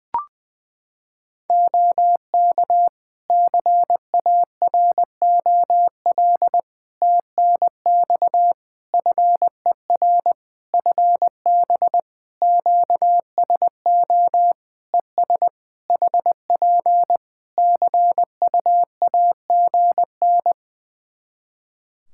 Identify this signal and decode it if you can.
Morse code, heard as a keyed tone, about 20 wpm, 700 Hz. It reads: OK CAROL TNX FER FB QSO ES HP CUAGN
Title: K1USN SST QSO Practice 20wpm